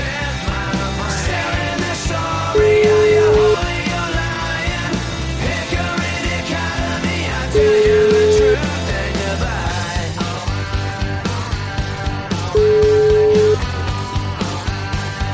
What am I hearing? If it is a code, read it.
Ringback tone (Europe, 425 Hz), with noise.